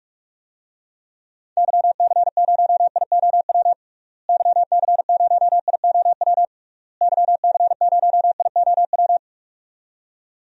Morse code, heard as a keyed tone, about 41 wpm, 700 Hz. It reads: YC0IOW YC0IOW YC0IOW
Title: DXCC-100 Call Signs - 6 Characters - Repeated 3x - 45wpm